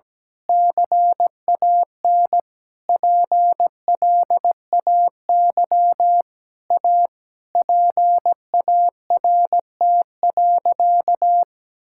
Morse code, heard as a keyed tone, about 17 wpm, 700 Hz. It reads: CAN PLAY A PART.
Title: Sentences from Top 1000 Words - Review Part 5 - 17wpm